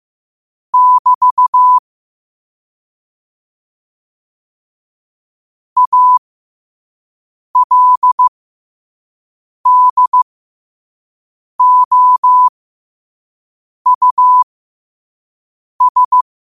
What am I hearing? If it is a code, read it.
Morse code, heard as a keyed tone, about 15 wpm, spaced as for 5 wpm, 1000 Hz. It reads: <BT> ALDOUS
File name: qotdfiles / 5wpm